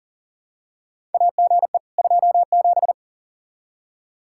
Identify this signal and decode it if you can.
Morse code, heard as a keyed tone, about 40 wpm, 700 Hz. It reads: AGE 27
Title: K1USN SST QSO Practice 40wpm